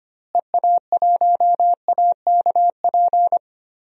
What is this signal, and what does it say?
Morse code, heard as a keyed tone, about 25 wpm, 700 Hz. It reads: EA1AKP
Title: DXCC-100 Call Signs - 6 Characters - Repeated 3x - 25wpm